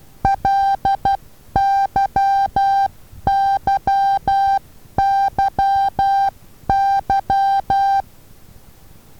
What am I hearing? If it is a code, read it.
Morse code, heard as a keyed tone, about 11 wpm, 790 Hz. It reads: LYYYY
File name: Taviro HA6VA B